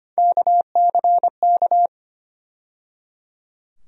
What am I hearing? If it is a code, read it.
Morse code, heard as a keyed tone, about 25 wpm, 700 Hz. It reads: KCK
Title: USA Call Sign Postfixes - 3 Letters 25wpm